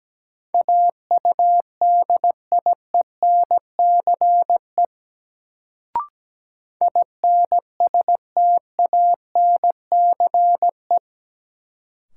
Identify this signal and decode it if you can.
Morse code, heard as a keyed tone, about 17 wpm, 700 Hz. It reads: AUDIENCE INSTANCE
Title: Sentences from Top 800 Words - New Word Drill - 17wpm